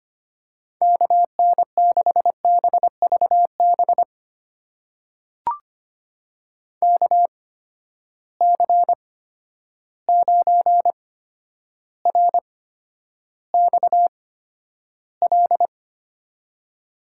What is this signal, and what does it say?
Morse code, heard as a keyed tone, about 25 wpm, spaced as for 11 wpm, 700 Hz. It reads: KN6BVB KC9RXL
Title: USA Call Signs - 6 Characters - 8x Character Spacing - 25wpm